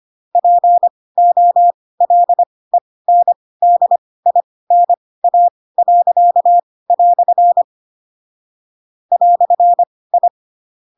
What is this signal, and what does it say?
Morse code, heard as a keyed tone, about 25 wpm, spaced as for 18 wpm, 700 Hz. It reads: POLENDINA." "I
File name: Pinocchio / Pinocchio ch2 12 25 ik1hge